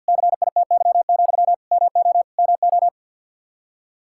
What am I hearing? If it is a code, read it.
Morse code, heard as a keyed tone, about 51 wpm, 700 Hz. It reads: CITY, MO MO
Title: Practice QSO - Exchange 1 - Beginning - 50wpm